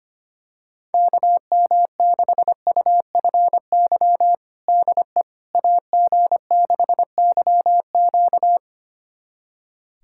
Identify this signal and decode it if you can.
Morse code, heard as a keyed tone, about 25 wpm, 700 Hz. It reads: KM6UFY DE AG6YQ
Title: USA Callsign DE Callsign 25wpm